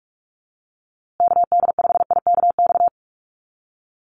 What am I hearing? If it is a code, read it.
Morse code, heard as a keyed tone, about 44 wpm, 700 Hz. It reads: KD5IKX
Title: USA Call Signs Repeated 2x 45wpm